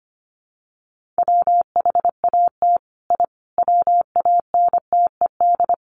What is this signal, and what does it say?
Morse code, heard as a keyed tone, about 25 wpm, 700 Hz. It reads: WHAT I WANTED
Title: Sets of 5 Words Encore2 25wpm